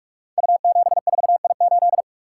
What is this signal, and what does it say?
Morse code, heard as a keyed tone, about 44 wpm, 700 Hz. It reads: U74I8
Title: Code Groups - 5 Letter:Numbers - 45wpm